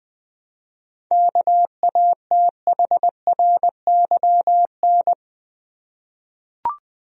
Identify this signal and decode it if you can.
Morse code, heard as a keyed tone, about 20 wpm, 700 Hz. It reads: KATHRYN
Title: USA Common Names - 20wpm